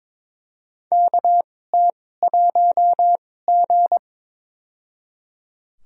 Morse code, heard as a keyed tone, about 21 wpm, spaced as for 16 wpm, 700 Hz. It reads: KT1G
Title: USA Call Signs - 4 Characters - 2x Character Spacing - 22wpm